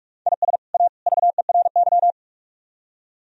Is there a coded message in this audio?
Yes: Morse code from 0.3 to 2.1 s, heard as a keyed tone, about 49 wpm, 700 Hz. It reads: IS A VERY